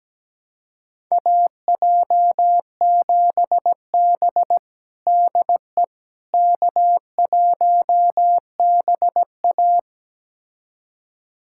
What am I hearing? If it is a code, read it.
Morse code, heard as a keyed tone, about 17 wpm, 700 Hz. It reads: AJ7B DE K1BA
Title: USA Callsign DE Callsign - 4 Characters - 17wpm